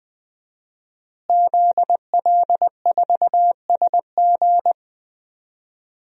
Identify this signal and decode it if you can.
Morse code, heard as a keyed tone, about 20 wpm, 700 Hz. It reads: ZL4SG
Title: DXCC-100 Call Signs - 5 Characters - Repeated 3x - 20wpm